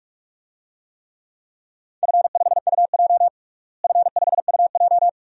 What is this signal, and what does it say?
Morse code, heard as a keyed tone, about 42 wpm, 700 Hz. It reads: F5UJ F5UJ
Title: DXCC-100 Call Signs - 4 Characters - Repeated 2x - 45wpm